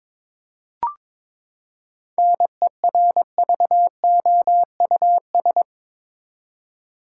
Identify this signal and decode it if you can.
Morse code, heard as a keyed tone, about 21 wpm, 700 Hz. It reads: NERVOUS
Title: Seven Letter Words - 22wpm